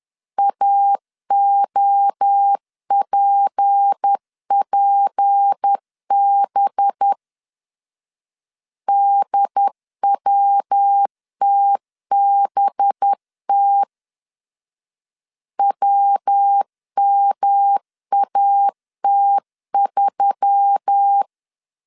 Morse code, heard as a keyed tone, about 10 wpm, 800 Hz. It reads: AOPPB DWTBT WMAT3